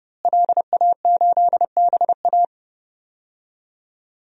Morse code, heard as a keyed tone, about 30 wpm, 700 Hz. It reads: LA8BA
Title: DXCC-100 Call Signs - 5 Characters - Repeated 2x - 30wpm